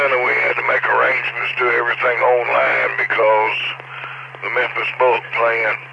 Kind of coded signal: Emergency Alert System (SAME) tones